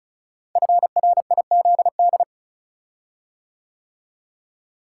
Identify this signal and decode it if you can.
Morse code, heard as a keyed tone, about 36 wpm, 700 Hz. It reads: FRIZD
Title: Code Groups - 5 Letter:Numbers - 35wpm